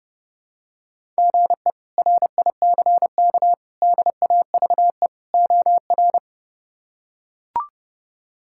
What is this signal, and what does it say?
Morse code, heard as a keyed tone, about 32 wpm, 700 Hz. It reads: GE RICK DAVE OR
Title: K1USN SST QSO Practice - 30wpm